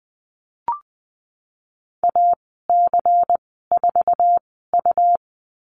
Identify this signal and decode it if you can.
Morse code, heard as a keyed tone, about 20 wpm, spaced as for 14 wpm, 700 Hz. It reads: AC4U
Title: 140 Lesson 23g - Letter f - Call Signs 20wpm